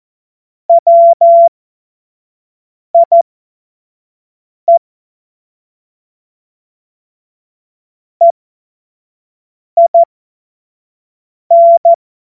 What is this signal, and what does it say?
Morse code, heard as a keyed tone, about 14 wpm, spaced as for 5 wpm, 680 Hz. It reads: WIE EIN
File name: Morsekurs / Witze / Witz 24.Oktober 2023 25 F70